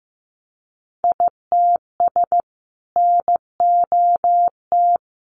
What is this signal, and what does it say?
Morse code, heard as a keyed tone, about 15 wpm, 700 Hz. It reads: ITS NOT
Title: Sentences from Top 600 Words 15wpm